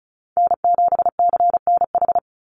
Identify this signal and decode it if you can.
Morse code, heard as a keyed tone, about 34 wpm, 700 Hz. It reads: N7CNH
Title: NA SOTA Chaser Callsigns 35wpm